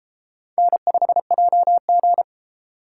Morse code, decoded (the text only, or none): N5JG